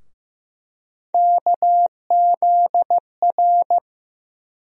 Morse code, heard as a keyed tone, about 15 wpm, 700 Hz. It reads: KZR
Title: USA Call Sign Postfixes - 3 Letters 15wpm